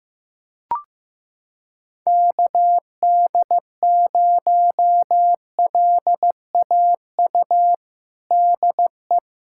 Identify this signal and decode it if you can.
Morse code, heard as a keyed tone, about 15 wpm, 700 Hz. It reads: KD0LAU DE
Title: USA Callsign DE Callsign 15wpm